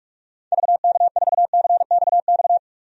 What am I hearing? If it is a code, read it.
Morse code, heard as a keyed tone, about 46 wpm, 700 Hz. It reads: VK4CXX
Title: DXCC-100 Call Signs - 6 Characters - 45wpm